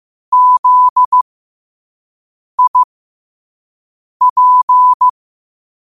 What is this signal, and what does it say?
Morse code, heard as a keyed tone, about 15 wpm, spaced as for 5 wpm, 1000 Hz. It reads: ZIP
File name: qotdfiles / 5wpm